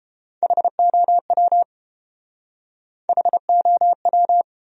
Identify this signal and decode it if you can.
Morse code, heard as a keyed tone, about 33 wpm, 700 Hz. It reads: HOW HOW